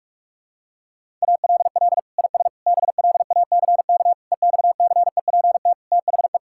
Morse code, heard as a keyed tone, about 41 wpm, 700 Hz. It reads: ALL IS BLACK EXCEPT THE
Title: Sentences from Top 1600 Words 45wpm